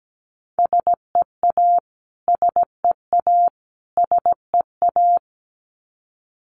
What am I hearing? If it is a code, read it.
Morse code, heard as a keyed tone, about 17 wpm, 700 Hz. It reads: SEA SEA SEA
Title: Top 500 Words Repeated 17wpm